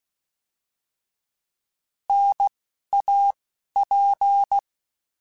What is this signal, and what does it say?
Morse code, heard as a keyed tone, about 16 wpm, 800 Hz. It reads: N A P
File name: Contrasts / Contrasts-mp3s / 3LtrContrasts @10